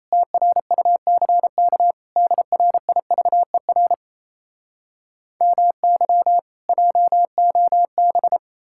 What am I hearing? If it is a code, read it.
Morse code, heard as a keyed tone, about 38 wpm, 700 Hz. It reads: TRUCK DRIVER MY JOB